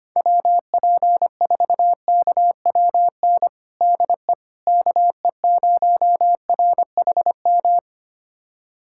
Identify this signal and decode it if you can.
Morse code, heard as a keyed tone, about 25 wpm, 700 Hz. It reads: WP4KWN DE KE0RHM
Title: USA Callsign DE Callsign - 6 Characters - 25wpm